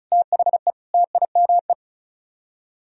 Morse code, heard as a keyed tone, about 46 wpm, 700 Hz. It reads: THE TIME